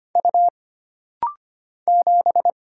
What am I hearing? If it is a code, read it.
Morse code, heard as a keyed tone, about 25 wpm, 700 Hz. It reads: U 7
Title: Single Letter-Number Alt - Rapid-Fire 25wpm